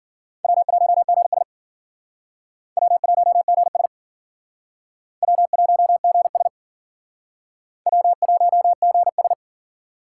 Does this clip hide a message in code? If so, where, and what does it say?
Yes: Morse code from 0.4 to 9.3 s, heard as a keyed tone, about 53 wpm, 700 Hz. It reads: W1GS W1GS W1GS W1GS